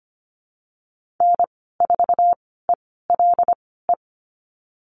Morse code, heard as a keyed tone, about 25 wpm, spaced as for 16 wpm, 700 Hz. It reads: N4ELE
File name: Lesson 15g - Number 5 - Call Signs 25wpm